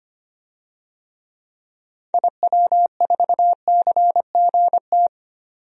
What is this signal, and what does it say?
Morse code, heard as a keyed tone, about 25 wpm, 700 Hz. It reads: IW4CGT
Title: DXCC-100 Call Signs - 6 Characters - 25wpm